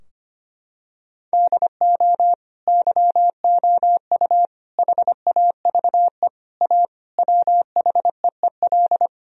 Morse code, heard as a keyed tone, about 25 wpm, 700 Hz. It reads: DO YOU HAVE A WHEEL
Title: Sentences from Top 1000 Words - Review Part 3 - 25wpm